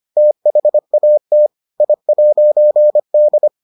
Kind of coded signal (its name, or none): Morse code